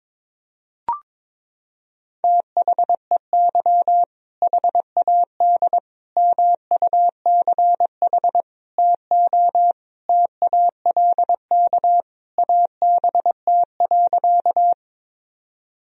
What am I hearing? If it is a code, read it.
Morse code, heard as a keyed tone, about 22 wpm, 700 Hz. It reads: THEY HAD MUCH TO TALK ABT.
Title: Sentences from Top 400 Words and Abbreviations 22wpm